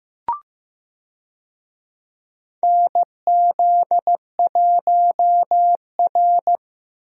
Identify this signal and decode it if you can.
Morse code, heard as a keyed tone, about 16 wpm, 700 Hz. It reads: NZ1R
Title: USA Call Signs - 4 Characters - Repeated 3x - 15wpm